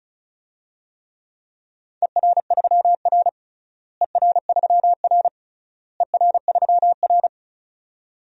Morse code, heard as a keyed tone, about 34 wpm, 700 Hz. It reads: ER3R ER3R ER3R
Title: DXCC-100 Call Signs - 4 Characters - Repeated 3x - 35wpm